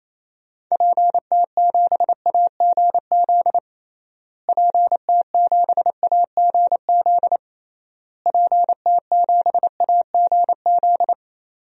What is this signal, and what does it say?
Morse code, heard as a keyed tone, about 28 wpm, 700 Hz. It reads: PT7AGZ PT7AGZ PT7AGZ
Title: DXCC-100 Call Signs - 6 Characters - Repeated 3x - 28wpm